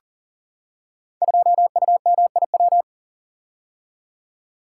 Morse code, heard as a keyed tone, about 38 wpm, 700 Hz. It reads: 2UMIW
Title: Code Groups - 5 Letter:Numbers - 40wpm